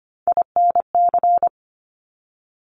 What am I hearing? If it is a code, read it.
Morse code, heard as a keyed tone, about 25 wpm, 700 Hz. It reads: INC
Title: Top Three Letter Combinations in Words 25wpm